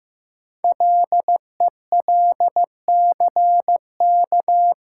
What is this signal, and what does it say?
Morse code, heard as a keyed tone, about 15 wpm, 700 Hz. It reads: LELCK
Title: Code Groups - 5 Letters - 15wpm